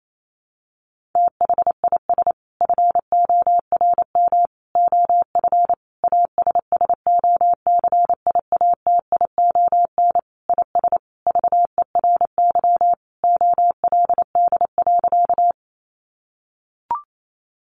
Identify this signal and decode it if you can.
Morse code, heard as a keyed tone, about 28 wpm, 700 Hz. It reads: THIS FORM OF ASSOCIATION IS VERY OLD.
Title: Sentences from Top 1100 Words 28wpm